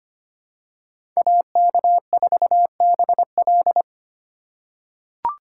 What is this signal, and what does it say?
Morse code, heard as a keyed tone, about 25 wpm, 700 Hz. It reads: AK4BL
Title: USA Call Signs - 5 Characters - 3x Character Spacing - 25wpm